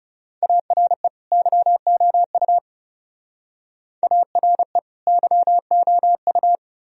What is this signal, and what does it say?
Morse code, heard as a keyed tone, about 35 wpm, 700 Hz. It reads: ARE YOU ARE YOU
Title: Speed-Racing - Sets of 2 Words 30wpm